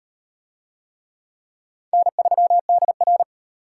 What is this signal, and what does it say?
Morse code, heard as a keyed tone, about 34 wpm, 700 Hz. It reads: N3DR